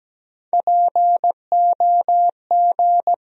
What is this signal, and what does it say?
Morse code, heard as a keyed tone, about 17 wpm, 700 Hz. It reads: POG
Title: USA Call Sign Postfixes - 3 Letters 17wpm